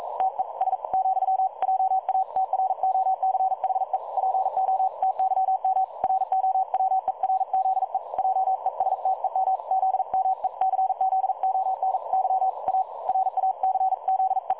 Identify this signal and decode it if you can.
Morse code, heard as a keyed tone, about 42 wpm, 760 Hz. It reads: TES , JA TROCHE PO TYM ROWERZE JESTEM ZMECZONY A RACZE
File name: dlazaawansowanych / nagraniazpasma / websdr recording 3565.0kHz